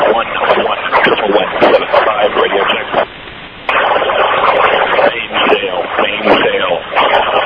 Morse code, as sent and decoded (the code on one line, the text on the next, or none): none